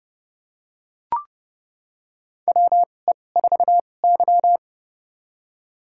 Morse code, heard as a keyed tone, about 31 wpm, spaced as for 22 wpm, 700 Hz. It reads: WE4Y
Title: USA Call Signs - 4 Characters - 2x Character Spacing - 30wpm